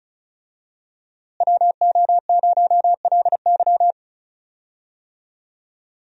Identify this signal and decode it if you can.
Morse code, heard as a keyed tone, about 34 wpm, 700 Hz. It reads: WO0LY
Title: USA Call Signs - 5 Characters - Repeated 3x - 35wpm